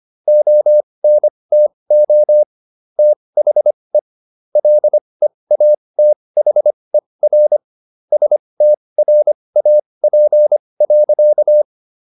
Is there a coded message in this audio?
Yes: Morse code from 0.3 to 11.6 s, heard as a keyed tone, about 25 wpm, spaced as for 20 wpm, 600 Hz. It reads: ONTO THE LEATHER STRAP.